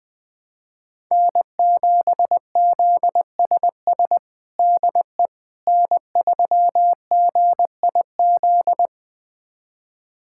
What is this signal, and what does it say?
Morse code, heard as a keyed tone, about 21 wpm, 700 Hz. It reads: N7ZSS DE N3GIZ
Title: USA Callsign DE Callsign - 5 Characters - 20wpm